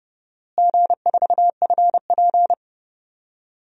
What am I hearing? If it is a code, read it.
Morse code, heard as a keyed tone, about 32 wpm, 700 Hz. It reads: G4FP
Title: DXCC-100 Call Signs - 4 Characters - Repeated 3x - 30wpm